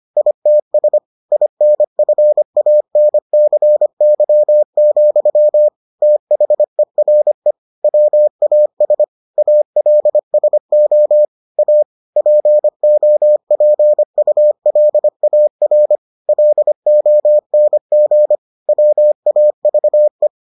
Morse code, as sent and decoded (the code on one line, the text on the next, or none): .. - ... / .. -. ..-. .- -. -.-. -.-- --..-- / - .... .-. / .-- .- ... / .- .-.. ... --- / .- / .--. --- .--. ..- .-.. .- .-. / .-.. --- -. --. / .-- .- ...- .
ITS INFANCY, THERE WAS ALSO A POPULAR LONG WAVE